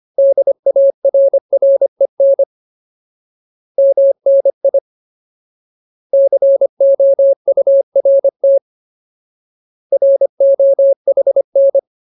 Morse code, heard as a keyed tone, about 25 wpm, 550 Hz. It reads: DARREN MNI COURT ROHN
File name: Ultimate Word List / 25WPM Ultimate